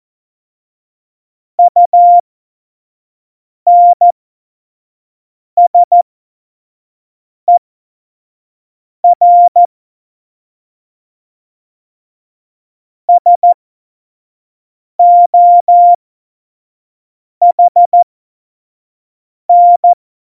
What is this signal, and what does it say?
Morse code, heard as a keyed tone, about 14 wpm, spaced as for 5 wpm, 700 Hz. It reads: UNSER SOHN
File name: Morsekurs / Witze / Witz 09. Januar 2018, 25 F70